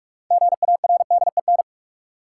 Morse code, heard as a keyed tone, about 41 wpm, 700 Hz. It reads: GARDEN
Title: Six Letter Words - 45wpm